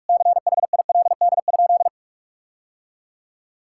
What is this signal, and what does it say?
Morse code, heard as a keyed tone, about 41 wpm, 700 Hz. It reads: CHILD?
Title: Sentences from Top 1600 Words 45wpm